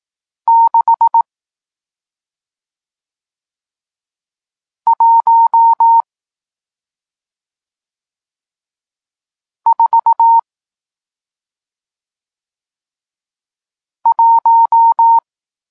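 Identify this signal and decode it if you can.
Morse code, heard as a keyed tone, about 18 wpm, spaced as for 2 wpm, 920 Hz. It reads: 6141
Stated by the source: Morse code lesson teaching digits 1, 4 and 6, from SP9MOA, Niepołomicki Klub Krótkofalowców (Poland) amateur radio club.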